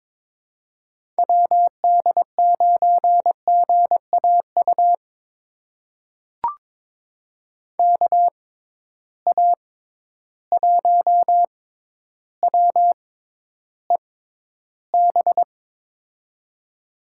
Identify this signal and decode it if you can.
Morse code, heard as a keyed tone, about 22 wpm, spaced as for 11 wpm, 700 Hz. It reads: WD9GAU KA1WEB
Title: USA Call Signs - 6 Characters - 6x Character Spacing - 22wpm